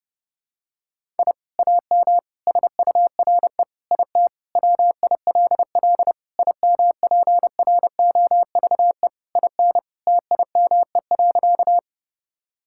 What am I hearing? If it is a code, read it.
Morse code, heard as a keyed tone, about 30 wpm, 700 Hz. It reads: I AM SURE IT WILL IMPROVE IN TIME.
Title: Sentences from Top 1000 Words - Review Part 5 - 30wpm